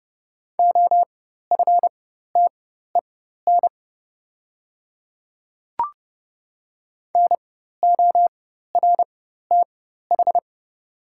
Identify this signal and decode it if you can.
Morse code, heard as a keyed tone, about 30 wpm, spaced as for 14 wpm, 700 Hz. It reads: OFTEN NORTH
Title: 4x Character Spacing - Top 300 Words - 30wpm